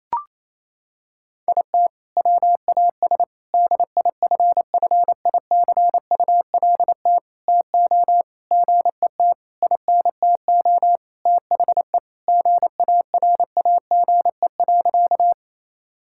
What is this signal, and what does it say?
Morse code, heard as a keyed tone, about 28 wpm, 700 Hz. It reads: IT WAS DIFFICULT TO GET INTO THE GARAGE.
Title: Sentences from Top 1600 Words 28wpm